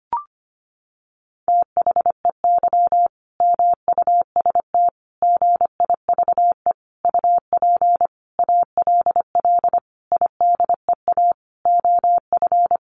Morse code, heard as a keyed tone, about 25 wpm, 700 Hz. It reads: THEY MUST GIVE UP ALL IDEA OF
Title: Sentences from Top 1100 Words 25wpm